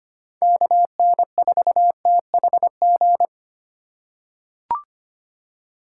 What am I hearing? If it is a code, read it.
Morse code, heard as a keyed tone, about 25 wpm, 690 Hz. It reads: KN4THG
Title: USA Call Signs - 6 Characters - 3x Character Spacing - 25wpm